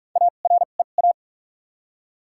Morse code, heard as a keyed tone, about 46 wpm, spaced as for 32 wpm, 700 Hz. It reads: AREA
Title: 2x Character Spacing - Top 500 Words - 45wpm